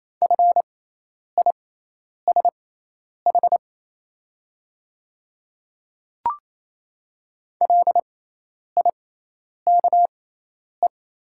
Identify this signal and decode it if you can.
Morse code, heard as a keyed tone, about 28 wpm, spaced as for 10 wpm, 700 Hz. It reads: FISH LIKE
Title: 6x Character Spacing - Top 300 Words - 28wpm